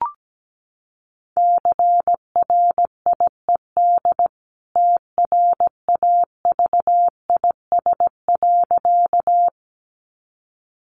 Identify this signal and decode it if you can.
Morse code, heard as a keyed tone, about 17 wpm, 700 Hz. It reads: CRIED TRAVIS.
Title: A Sound Of Thunder 17wpm